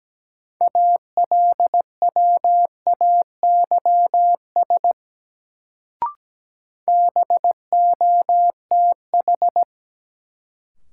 Morse code, heard as a keyed tone, about 17 wpm, 700 Hz. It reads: ALWAYS BOTH
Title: Sentences from Top 300 Words - New Word Drill - 17wpm